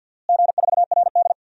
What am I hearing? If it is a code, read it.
Morse code, heard as a keyed tone, about 48 wpm, 700 Hz. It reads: G4RD
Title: DXCC-100 Call Signs - 4 Characters - Repeated 3x - 50wpm